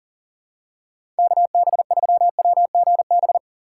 Morse code, heard as a keyed tone, about 40 wpm, 700 Hz. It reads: KB3WGB
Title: USA Call Signs - 6 Characters - 3x Character Spacing - 40wpm